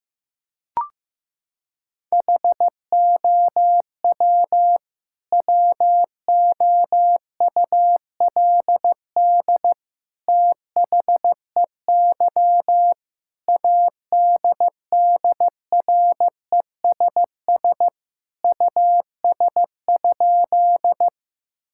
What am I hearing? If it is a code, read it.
Morse code, heard as a keyed tone, about 15 wpm, 700 Hz. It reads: HOW WOULD THEY ADDRESS US?
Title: Sentences from Top 1000 Words - Review Part 6 - 15wpm